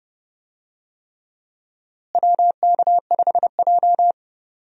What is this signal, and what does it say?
Morse code, heard as a keyed tone, about 31 wpm, 700 Hz. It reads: WK5J
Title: USA Call Signs - 4 Characters - Repeated 3x - 30wpm